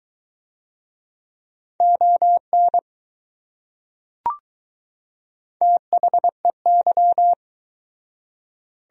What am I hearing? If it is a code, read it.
Morse code, heard as a keyed tone, about 21 wpm, 700 Hz. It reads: ON THEY